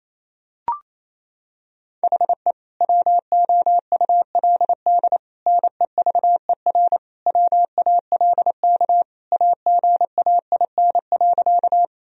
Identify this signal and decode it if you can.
Morse code, heard as a keyed tone, about 28 wpm, 700 Hz. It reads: HE WOULD NEVER WALK AGAIN.
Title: Sentences from Top 1000 Words - Review Part 1 - 28wpm